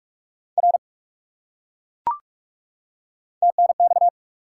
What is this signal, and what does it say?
Morse code, heard as a keyed tone, about 40 wpm, 700 Hz. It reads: R TNX